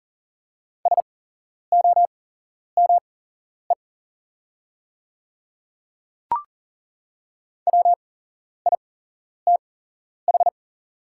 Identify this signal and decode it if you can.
Morse code, heard as a keyed tone, about 40 wpm, spaced as for 11 wpm, 690 Hz. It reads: SOME WITH
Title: 8x Character Spacing - Top 100 Words - 40wpm